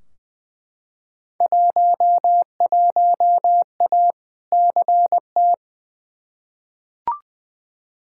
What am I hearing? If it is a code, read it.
Morse code, heard as a keyed tone, about 20 wpm, 700 Hz. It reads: 11A CT
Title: Field Day Exchanges Encore - 20wpm